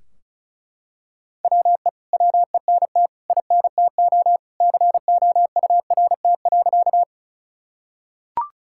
Morse code, heard as a keyed tone, about 36 wpm, 700 Hz. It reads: WE WENT INTO COURT.
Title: Sentences from Top 1600 Words 35wpm